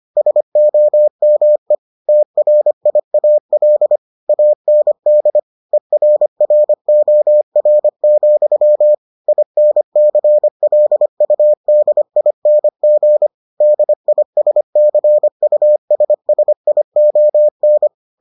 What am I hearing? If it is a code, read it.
Morse code, heard as a keyed tone, about 25 wpm, 600 Hz. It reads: SOME TRIAL AND ERROR, INCLUDING DISCUSSION